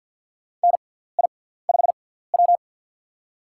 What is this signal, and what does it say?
Morse code, heard as a keyed tone, about 51 wpm, spaced as for 17 wpm, 700 Hz. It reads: NI5W
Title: USA Call Signs - 6x Character Spacing - 50wpm